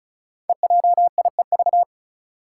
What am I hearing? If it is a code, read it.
Morse code, heard as a keyed tone, about 36 wpm, 700 Hz. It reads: EJIEV
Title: Five Challenging Letter-Number Combinations 35wpm